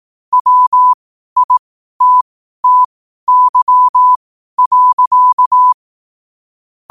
Morse code, heard as a keyed tone, about 18 wpm, spaced as for 12 wpm, 1000 Hz. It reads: WITTY.
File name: qotdfiles / 12wpm